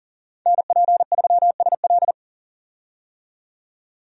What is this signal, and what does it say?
Morse code, heard as a keyed tone, about 38 wpm, 700 Hz. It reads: NP3SL
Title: USA Call Signs - 5 Characters - 40wpm